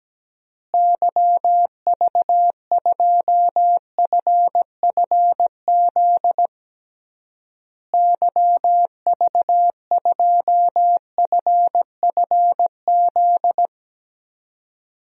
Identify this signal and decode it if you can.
Morse code, heard as a keyed tone, about 17 wpm, 700 Hz. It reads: YV2FFZ YV2FFZ
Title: DXCC-100 Call Signs - 6 Characters - Repeated 3x - 17wpm